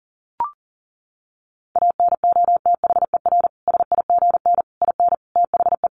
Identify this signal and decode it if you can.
Morse code, heard as a keyed tone, about 40 wpm, 700 Hz. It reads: ANOTHER SIGN IN THE
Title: Sentences from Top 1000 Words 40wpm